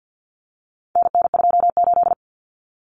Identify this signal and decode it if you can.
Morse code, heard as a keyed tone, about 50 wpm, 700 Hz. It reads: NN28
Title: Common Cut Numbers - 4-Digit 50wpm